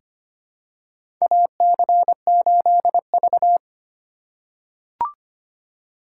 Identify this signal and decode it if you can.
Morse code, heard as a keyed tone, about 25 wpm, 700 Hz. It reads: AC8V
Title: USA Call Signs - 4 Characters - 4x Character Spacing - 25wpm